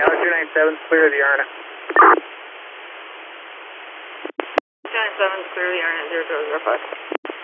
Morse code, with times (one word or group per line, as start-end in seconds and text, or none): none